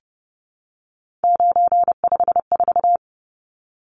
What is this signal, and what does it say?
Morse code, heard as a keyed tone, about 30 wpm, 700 Hz. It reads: 954
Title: Numbers - 3-Digit 30wpm